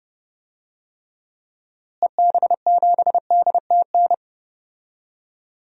Morse code, heard as a keyed tone, about 32 wpm, 700 Hz. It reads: EB7DTN
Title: DXCC-100 Call Signs - 6 Characters - Repeated 2x - 30wpm